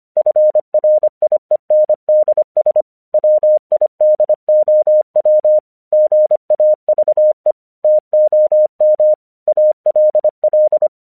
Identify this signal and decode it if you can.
Morse code, heard as a keyed tone, about 25 wpm, 610 Hz. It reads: FRIENDS WIDOW GAVE TOM ALL
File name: 1 English 25wpm / AQC12